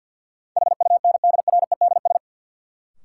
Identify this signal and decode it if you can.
Morse code, heard as a keyed tone, about 53 wpm, 700 Hz. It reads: HUNDREDS
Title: Sets of 4 Words - New Word Drill - 50wpm